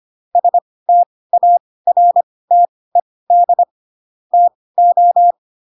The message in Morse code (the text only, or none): STARTED TO